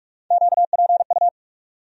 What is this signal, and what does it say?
Morse code, heard as a keyed tone, about 41 wpm, 700 Hz. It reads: QPU
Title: USA Call Sign Postfixes - 3 Letters 45wpm